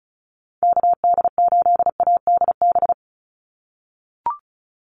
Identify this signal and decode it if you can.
Morse code, heard as a keyed tone, about 34 wpm, 700 Hz. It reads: KD8ADB
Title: USA Call Signs 35wpm